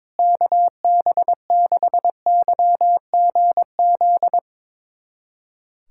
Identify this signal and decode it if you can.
Morse code, heard as a keyed tone, about 22 wpm, 700 Hz. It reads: KB6YGZ
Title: USA Callsign DE Callsign - 6 Characters - 22wpm